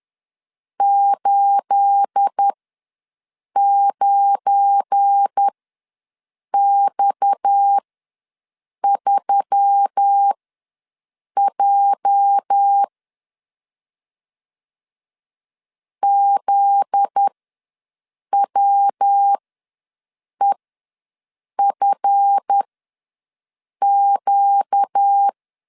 Morse code, heard as a keyed tone, about 10 wpm, spaced as for 6 wpm, 800 Hz. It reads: 89X3J ZWEFQ
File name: LEZIONI / 9 28 ws 10wpm 4124